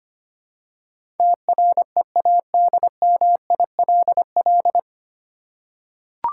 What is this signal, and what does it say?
Morse code, heard as a keyed tone, about 25 wpm, 700 Hz. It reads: TREADMILL
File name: practice / 25wpm / Easy Words with Hints - 25wpm